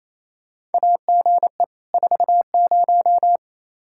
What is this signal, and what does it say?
Morse code, heard as a keyed tone, about 27 wpm, 700 Hz. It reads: AGE 40
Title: Practice QSO - Exchange 3 - More Info - 28wpm